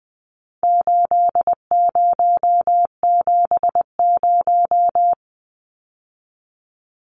Morse code, heard as a keyed tone, about 20 wpm, 700 Hz. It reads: 8070
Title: Numbers - 4-Digit 20wpm